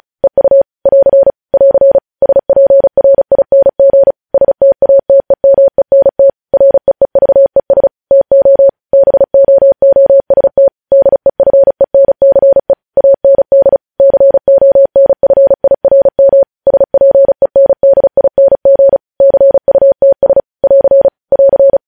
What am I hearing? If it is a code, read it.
Morse code, heard as a keyed tone, about 36 wpm, 550 Hz. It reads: EU <AR> <AR> SPRING STATEMENT REEVES TO BOOST DEFENCE AND CONFIRM SPENDING CUTS <AR> <AR>
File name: podcast / News headlines-1x-35x35